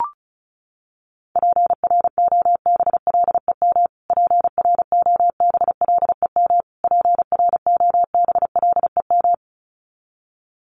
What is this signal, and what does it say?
Morse code, heard as a keyed tone, about 34 wpm, 700 Hz. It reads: PROBLEM PROBLEM PROBLEM
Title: Top 500 Words Repeated 35wpm